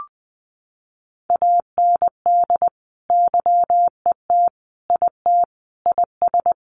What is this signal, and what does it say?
Morse code, heard as a keyed tone, about 20 wpm, 700 Hz. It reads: AND YET IT IS
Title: Sentences from Top 1400 Words 20wpm